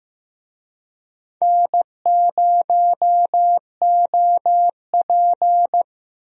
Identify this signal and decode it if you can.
Morse code, heard as a keyed tone, about 15 wpm, 700 Hz. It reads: N0OP
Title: USA Call Signs - 4 Characters - Repeated 2x - 15wpm